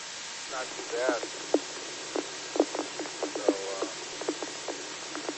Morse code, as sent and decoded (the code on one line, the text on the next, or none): none